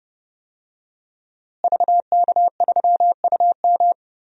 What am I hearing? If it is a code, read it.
Morse code, heard as a keyed tone, about 30 wpm, 700 Hz. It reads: VK3UM
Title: DXCC-100 Call Signs - 5 Characters - Repeated 3x - 30wpm